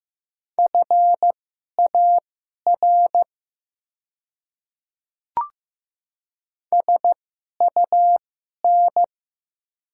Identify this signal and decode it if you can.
Morse code, heard as a keyed tone, about 15 wpm, spaced as for 11 wpm, 700 Hz. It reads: FAR SUN
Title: 2x Character Spacing - Top 300 Words - 15wpm